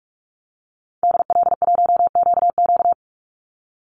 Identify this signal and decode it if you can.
Morse code, heard as a keyed tone, about 44 wpm, 700 Hz. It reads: DL1QQ
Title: Call Signs Repeated 2x - 45wpm